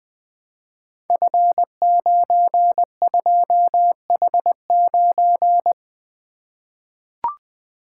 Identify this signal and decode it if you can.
Morse code, heard as a keyed tone, about 21 wpm, 700 Hz. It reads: F92H9
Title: Code Groups - 5 Letter:Numbers - 20wpm